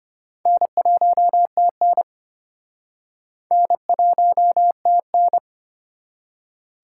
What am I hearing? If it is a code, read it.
Morse code, heard as a keyed tone, about 32 wpm, 700 Hz. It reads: N1TN N1TN